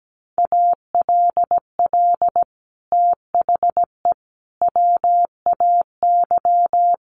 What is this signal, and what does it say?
Morse code, heard as a keyed tone, about 17 wpm, 700 Hz. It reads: ALL THE WAY
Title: Sets of 5 Words Encore 17wpm